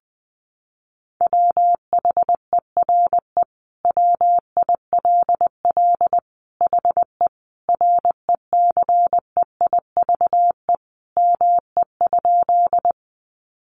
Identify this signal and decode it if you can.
Morse code, heard as a keyed tone, about 20 wpm, 700 Hz. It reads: WHERE WILL HE RECEIVE ME?
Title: Sentences from Top 1000 Words 20wpm